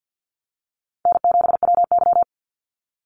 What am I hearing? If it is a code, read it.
Morse code, heard as a keyed tone, about 50 wpm, 700 Hz. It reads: N7WY
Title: Easy Call Signs Repeated 2x - 50wpm